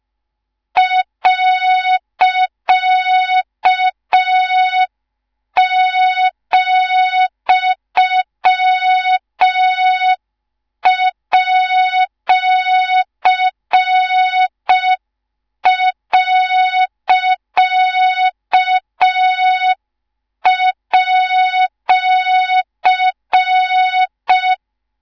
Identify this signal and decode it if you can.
Morse code, heard as a keyed tone, about 5 wpm, 3700 Hz, with noise. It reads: .,@.@